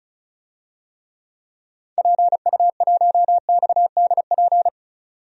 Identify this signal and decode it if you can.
Morse code, heard as a keyed tone, about 36 wpm, 700 Hz. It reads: PU1XDP
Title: DXCC-100 Call Signs - 6 Characters - 35wpm